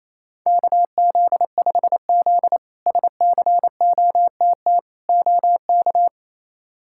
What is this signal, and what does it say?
Morse code, heard as a keyed tone, about 28 wpm, 700 Hz. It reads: KZ5Z SCOTT OK
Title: K1USN SST QSO Practice - 28wpm